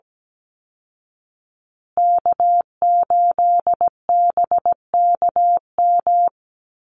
Morse code, heard as a keyed tone, about 17 wpm, 700 Hz. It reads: K8BKM
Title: Call Signs Repeated 2x - 17wpm